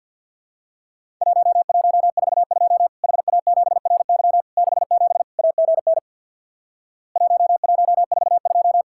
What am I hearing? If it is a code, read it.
Morse code, heard as a keyed tone, about 50 wpm, 700 Hz. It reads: 1142 HA7RY 67 AGN 1142
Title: Worked All Europe DX Contest - QTC Exchange with Repeats - 50wpm